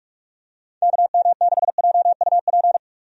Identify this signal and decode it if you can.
Morse code, heard as a keyed tone, about 46 wpm, 700 Hz. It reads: KM6JUP
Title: USA Call Signs - 6 Characters - 45wpm